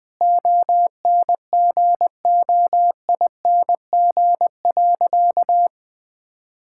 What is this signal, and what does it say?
Morse code, heard as a keyed tone, about 20 wpm, 700 Hz. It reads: ONGOING.
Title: Sentences with 3-Word Phrases - 20wpm